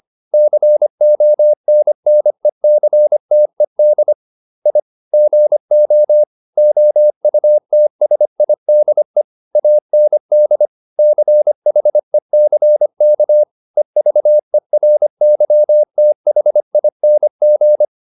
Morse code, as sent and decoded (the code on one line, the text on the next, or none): -.-. --- -. -. -.-. - . -.. / .. / --. --- / --- ..- - ... .. -.. / .- -. -.. / -.-. .... -.-. -.- / . ...- . .-. -.-- - .... .. -. --.
CONNECTED I GO OUTSIDE AND CHECK EVERYTHING